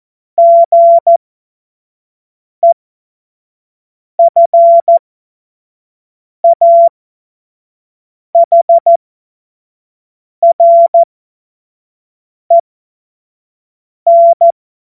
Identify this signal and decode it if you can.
Morse code, heard as a keyed tone, about 14 wpm, spaced as for 5 wpm, 680 Hz. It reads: GEFAHREN
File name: Witz-05-August-2025-25-F70